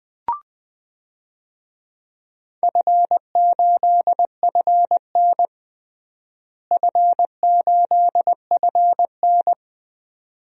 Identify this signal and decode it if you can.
Morse code, heard as a keyed tone, about 20 wpm, 700 Hz. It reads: F8FN F8FN
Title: DXCC-100 Call Signs - 4 Characters - Repeated 2x - 20wpm